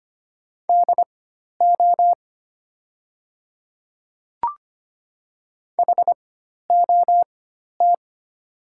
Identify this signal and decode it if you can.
Morse code, heard as a keyed tone, about 25 wpm, spaced as for 12 wpm, 700 Hz. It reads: DO HOT
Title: 4x Character Spacing - Top 100 Words - 25wpm